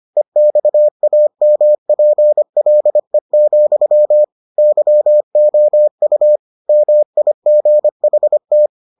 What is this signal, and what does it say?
Morse code, heard as a keyed tone, about 25 wpm, 600 Hz. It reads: EXAMPLE, YOU MIGHT